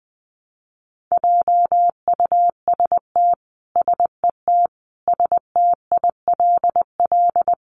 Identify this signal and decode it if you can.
Morse code, heard as a keyed tone, about 20 wpm, 700 Hz. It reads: JUST SET STILL
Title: Sentences from Top 300 Words 20wpm